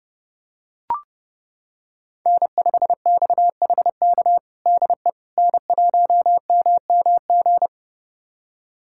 Morse code, heard as a keyed tone, about 30 wpm, 700 Hz. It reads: N5XHK DE N1MMG
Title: USA Callsign DE Callsign - 5 Characters - 30wpm